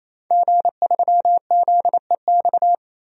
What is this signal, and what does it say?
Morse code, heard as a keyed tone, about 28 wpm, 700 Hz. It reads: G3ZEX
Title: DXCC-100 Call Signs - 5 Characters - 28wpm